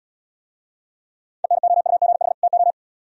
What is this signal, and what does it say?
Morse code, heard as a keyed tone, about 80 wpm, 700 Hz. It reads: ENOUGH TO